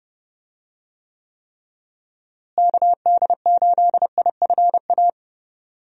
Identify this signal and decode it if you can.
Morse code, heard as a keyed tone, about 30 wpm, 700 Hz. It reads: KD8IFA
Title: USA Call Signs - 6 Characters - Repeated 3x - 30wpm